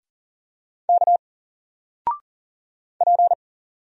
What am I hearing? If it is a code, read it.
Morse code, heard as a keyed tone, about 40 wpm, 700 Hz. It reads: K P
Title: Single Letter-Number 40wpm